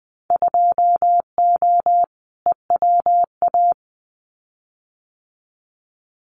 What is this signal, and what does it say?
Morse code, heard as a keyed tone, about 20 wpm, 700 Hz. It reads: 2O EWA
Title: Winter Field Day Exchanges Encore 20wpm